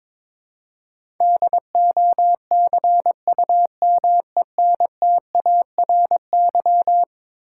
Morse code, heard as a keyed tone, about 22 wpm, 700 Hz. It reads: DOCUMENTARY
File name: practice / 22wpm / Eleven Letter Words - 22wpm